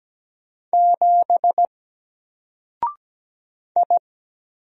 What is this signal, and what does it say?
Morse code, heard as a keyed tone, about 17 wpm, 700 Hz. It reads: 7 I
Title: Single Letter-Number 17wpm